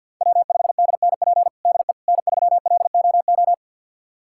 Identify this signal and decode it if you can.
Morse code, heard as a keyed tone, about 51 wpm, 700 Hz. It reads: W5DNP DE N3LOO
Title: USA Callsign DE Callsign - 5 Characters - 50wpm